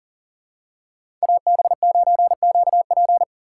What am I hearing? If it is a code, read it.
Morse code, heard as a keyed tone, about 40 wpm, 700 Hz. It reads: AB9QP
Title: USA Call Signs - 5 Characters - 2x Character Spacing - 40wpm